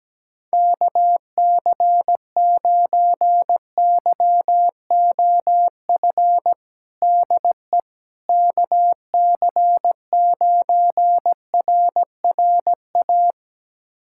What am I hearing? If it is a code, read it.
Morse code, heard as a keyed tone, about 17 wpm, 700 Hz. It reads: KC9YOF DE KC9RRA
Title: USA Callsign DE Callsign 17wpm